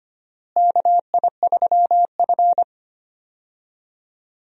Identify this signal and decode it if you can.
Morse code, heard as a keyed tone, about 25 wpm, 700 Hz. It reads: KI3F
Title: USA Call Signs - 4 Characters - Repeated 2x - 25wpm